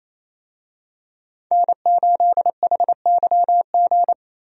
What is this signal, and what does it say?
Morse code, heard as a keyed tone, about 28 wpm, 700 Hz. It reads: N8HYG
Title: USA Call Signs - 5 Characters - Repeated 2x - 28wpm